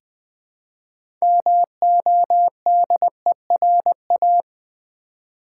Morse code, heard as a keyed tone, about 20 wpm, 700 Hz. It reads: MODERA
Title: Build A Word - 3 Letters At A Time - 8-Letter Words - 20wpm